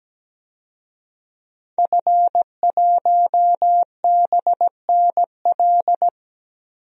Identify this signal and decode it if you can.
Morse code, heard as a keyed tone, about 17 wpm, 700 Hz. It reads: F1BNL
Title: DXCC-100 Call Signs - 5 Characters - Repeated 2x - 17wpm